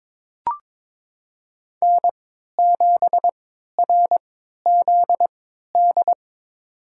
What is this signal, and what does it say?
Morse code, heard as a keyed tone, about 22 wpm, spaced as for 12 wpm, 700 Hz. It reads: N7RZD
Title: USA Call Signs - 5 Characters - 3x Character Spacing - 22wpm